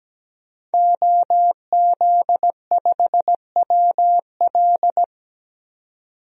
Morse code, heard as a keyed tone, about 17 wpm, 700 Hz. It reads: OZ5WL
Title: DXCC-100 Call Signs - 5 Characters - 17wpm